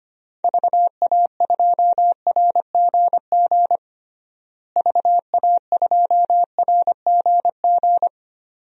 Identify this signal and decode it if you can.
Morse code, heard as a keyed tone, about 25 wpm, 700 Hz. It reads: VA2RGG VA2RGG
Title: DXCC-100 Call Signs - 6 Characters - Repeated 2x - 25wpm